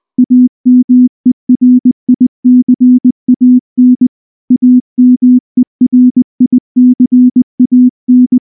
Morse code, heard as a keyed tone, about 19 wpm, 250 Hz. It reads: AMERICAN AMERICAN
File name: morse / morse-training-20WPM-6